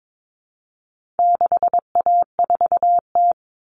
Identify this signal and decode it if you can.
Morse code, heard as a keyed tone, about 22 wpm, 700 Hz. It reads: 6A4T
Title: All Cut Numbers - 4-Digit 22wpm